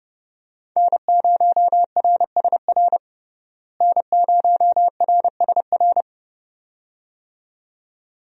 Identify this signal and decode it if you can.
Morse code, heard as a keyed tone, about 30 wpm, 700 Hz. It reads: N0RSR N0RSR
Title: USA Call Signs - 5 Characters - Repeated 2x - 30wpm